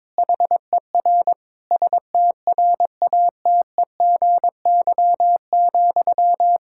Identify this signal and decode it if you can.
Morse code, heard as a keyed tone, about 22 wpm, 700 Hz. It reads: HER STRATEGY,
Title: Sentences from Top 1000 Words - Review Part 5 - 22wpm